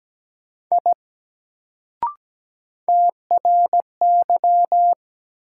Morse code, heard as a keyed tone, about 17 wpm, 700 Hz. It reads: I TRY
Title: Sentences from Top 200 Words - New Word Drill - 17wpm